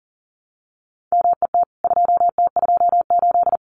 Morse code, heard as a keyed tone, about 40 wpm, 700 Hz. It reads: MET 2T28
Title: CWT Exchange 40wpm